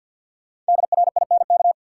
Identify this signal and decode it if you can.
Morse code, heard as a keyed tone, about 51 wpm, 700 Hz. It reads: DRINK
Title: Sentences from Top 1500 Words - New Word Drill - 50wpm